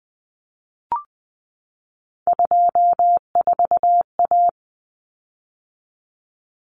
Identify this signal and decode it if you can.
Morse code, heard as a keyed tone, about 20 wpm, 700 Hz. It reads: 24A
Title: Common Cut Numbers - 3-Digit 20wpm